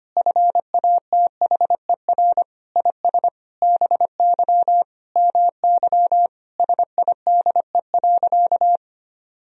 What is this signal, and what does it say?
Morse code, heard as a keyed tone, about 25 wpm, 700 Hz. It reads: FATHER IS BY MY SIDE.
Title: Sentences from Top 1000 Words - Review Part 1 - 25wpm